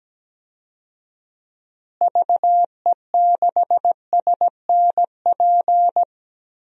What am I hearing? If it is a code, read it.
Morse code, heard as a keyed tone, about 17 wpm, 700 Hz. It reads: VE6SNP
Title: DXCC-100 Call Signs - 6 Characters - Repeated 2x - 17wpm